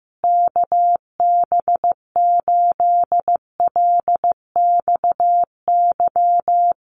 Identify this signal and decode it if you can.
Morse code, heard as a keyed tone, about 15 wpm, 700 Hz. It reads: KB8LXY
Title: Common QSO Phrases Encore 15wpm